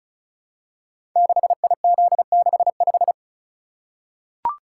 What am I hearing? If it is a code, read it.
Morse code, heard as a keyed tone, about 36 wpm, 690 Hz. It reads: 6IZ65
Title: Code Groups - 5 Letter:Numbers - 35wpm